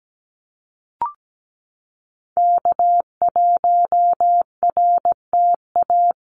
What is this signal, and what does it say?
Morse code, heard as a keyed tone, about 17 wpm, 700 Hz. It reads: K1RTA
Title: USA Call Signs Repeated 2x 17wpm